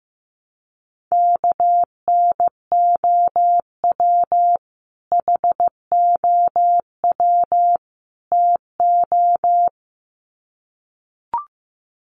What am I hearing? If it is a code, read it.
Morse code, heard as a keyed tone, about 15 wpm, 700 Hz. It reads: KNOW HOW TO
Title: Sets of 3 Words 15wpm